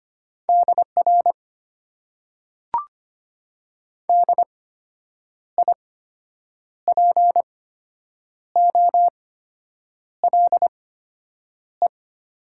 Morse code, heard as a keyed tone, about 25 wpm, spaced as for 8 wpm, 700 Hz. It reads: DR DIPOLE
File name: Top 90 QSO Elements - 8x Character Spacing - 25wpm